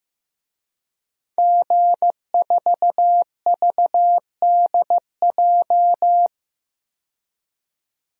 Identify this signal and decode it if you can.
Morse code, heard as a keyed tone, about 15 wpm, 700 Hz. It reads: G4VDJ
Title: DXCC-100 Call Signs - 5 Characters - 15wpm